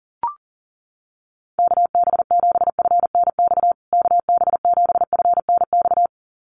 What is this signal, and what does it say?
Morse code, heard as a keyed tone, about 40 wpm, 700 Hz. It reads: KB7FNX KB7FNX
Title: USA Call Signs Repeated 2x 40wpm